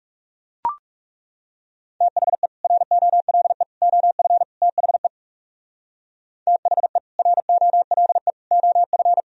Morse code, heard as a keyed tone, about 38 wpm, 700 Hz. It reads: THE ROLE OF THE THE ROLE OF